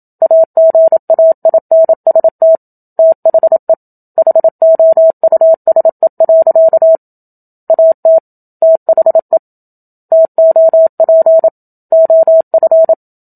Morse code, heard as a keyed tone, about 27 wpm, 650 Hz. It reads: AGAINST THE HOUSE. AT THE TOP OF
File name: Book Two / 01-Chapter 1 - 27 WPM